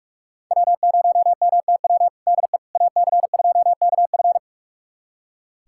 Morse code, heard as a keyed tone, about 41 wpm, 700 Hz. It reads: W0MTW DE AC2KF
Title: USA Callsign DE Callsign - 5 Characters - 45wpm